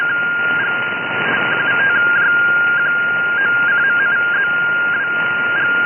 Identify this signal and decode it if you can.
Two-tone teleprinter signal, about 200 Hz shift.
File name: 16316 kHz